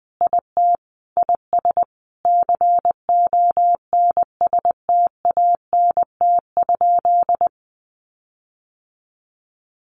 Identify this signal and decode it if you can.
Morse code, heard as a keyed tone, about 20 wpm, 700 Hz. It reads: IT IS CONSTANT?
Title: Sentences from Top 1500 Words 20wpm